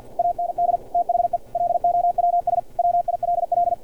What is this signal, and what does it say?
Morse code, heard as a keyed tone, about 50 wpm, 700 Hz. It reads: AND THE LOWS WILL
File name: archived / CRSnet